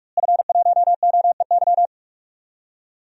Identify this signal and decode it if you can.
Morse code, heard as a keyed tone, about 46 wpm, 700 Hz. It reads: F1OEY
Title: DXCC-100 Call Signs - 5 Characters - Repeated 3x - 45wpm